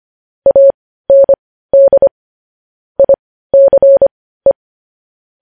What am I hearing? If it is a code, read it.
Morse code, heard as a keyed tone, about 25 wpm, spaced as for 15 wpm, 550 Hz. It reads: AND ICE
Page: News Headlines in Morse Code at 15 WPM